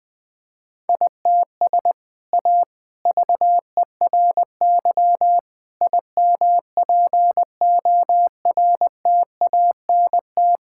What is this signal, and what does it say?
Morse code, heard as a keyed tone, about 20 wpm, 700 Hz. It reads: ITS A VERY IMPORTANT
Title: Sentences from Top 1000 Words - Review Part 5 - 20wpm